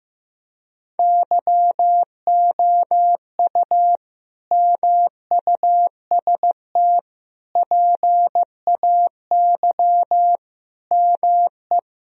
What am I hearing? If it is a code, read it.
Morse code, heard as a keyed tone, about 15 wpm, 700 Hz. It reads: YOU MUST PAY ME
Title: Sentences from Top 1000 Words - Review Part 2 - 15wpm